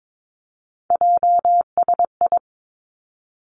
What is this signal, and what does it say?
Morse code, heard as a keyed tone, about 22 wpm, 700 Hz. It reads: JSI
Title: Three Challenging Letter-Number Combinations 22wpm